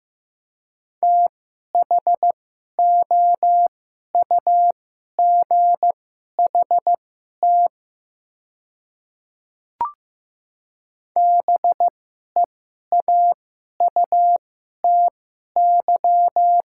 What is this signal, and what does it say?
Morse code, heard as a keyed tone, about 15 wpm, spaced as for 11 wpm, 700 Hz. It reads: THOUGHT BEAUTY
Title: 2x Character Spacing - Top 500 Words - 15wpm